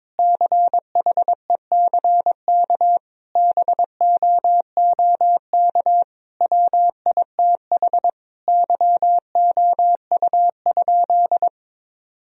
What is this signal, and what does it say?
Morse code, heard as a keyed tone, about 22 wpm, 700 Hz. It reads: CHECK BOOK WITH YOU?
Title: Sentences from Top 1000 Words - Review Part 3 - 22wpm